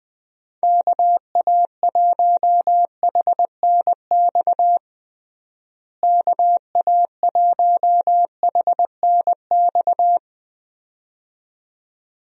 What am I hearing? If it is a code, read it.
Morse code, heard as a keyed tone, about 21 wpm, 700 Hz. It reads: KA1HNX KA1HNX
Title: USA Call Signs - 6 Characters - Repeated 2x - 20wpm